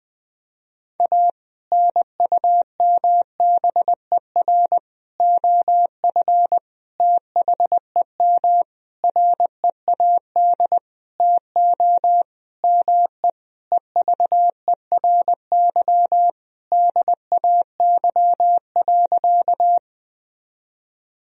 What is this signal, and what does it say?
Morse code, heard as a keyed tone, about 20 wpm, 700 Hz. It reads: A NUMBER OF THEM READ TO ME EVERY DAY.
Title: Sentences from Top 1000 Words - Review Part 1 - 20wpm